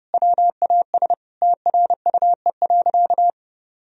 Morse code, heard as a keyed tone, about 30 wpm, 700 Hz. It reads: WAS TRUE.
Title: Sentences from Top 400 Words and Abbreviations 30wpm